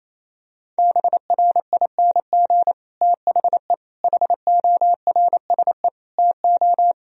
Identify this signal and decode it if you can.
Morse code, heard as a keyed tone, about 28 wpm, 700 Hz. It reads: BRING THE HORSE TO
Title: Sentences from Top 1000 Words - Review Part 3 - 28wpm